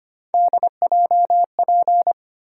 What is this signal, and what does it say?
Morse code, heard as a keyed tone, about 25 wpm, 700 Hz. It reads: DJP
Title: USA Call Sign Postfixes - 3 Letters 25wpm